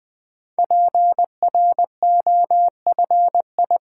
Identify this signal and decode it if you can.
Morse code, heard as a keyed tone, about 20 wpm, 700 Hz. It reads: PROFI
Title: Build A Word - 5 Letters At A Time - 10-Letter Words - 20wpm